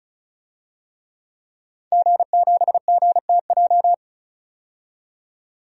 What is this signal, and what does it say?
Morse code, heard as a keyed tone, about 34 wpm, 700 Hz. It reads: G7GTJ
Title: DXCC-100 Call Signs - 5 Characters - 35wpm